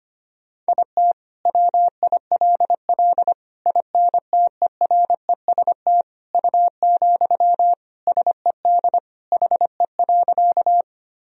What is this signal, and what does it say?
Morse code, heard as a keyed tone, about 25 wpm, 700 Hz. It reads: IT WILL INTEREST U, SED HE.
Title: Sentences from Top 400 Words and Abbreviations 25wpm